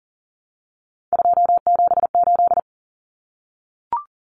Morse code, heard as a keyed tone, about 40 wpm, 700 Hz. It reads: 278
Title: Common Cut Numbers - 3-Digit 40wpm